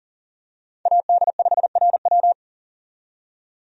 Morse code, heard as a keyed tone, about 40 wpm, 700 Hz. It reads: AD5RW
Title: USA Call Signs - 5 Characters - 40wpm